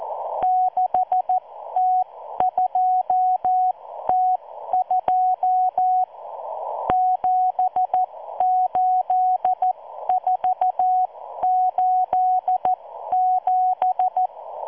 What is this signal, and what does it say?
Morse code, heard as a keyed tone, about 14 wpm, spaced as for 12 wpm, 740 Hz. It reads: 6T2T2 78487